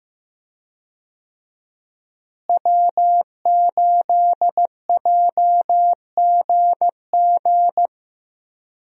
Morse code, heard as a keyed tone, about 15 wpm, 700 Hz. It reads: W8JGG
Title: USA Call Signs - 5 Characters - Repeated 3x - 15wpm